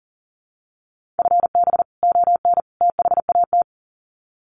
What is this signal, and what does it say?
Morse code, heard as a keyed tone, about 40 wpm, 700 Hz. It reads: FB ON THAT
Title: Common QSO Phrases 40wpm